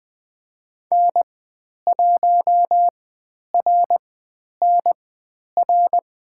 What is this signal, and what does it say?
Morse code, heard as a keyed tone, about 20 wpm, spaced as for 10 wpm, 700 Hz. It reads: N1RNR
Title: 095 Lesson 16g - Letter c - Call Signs 20wpm